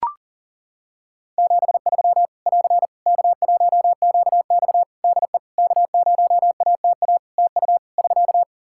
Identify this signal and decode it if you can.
Morse code, heard as a keyed tone, about 40 wpm, 700 Hz. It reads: 73 <AR> K1QX DE K0ATA TU <SK>
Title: Practice QSO - Exchange 4 - Ending - 40wpm